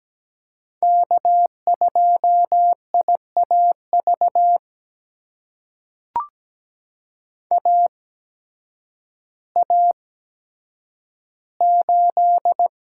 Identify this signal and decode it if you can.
Morse code, heard as a keyed tone, about 17 wpm, spaced as for 9 wpm, 700 Hz. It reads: K2IAV AA8
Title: USA Call Signs - 5 Characters - 8x Character Spacing - 17wpm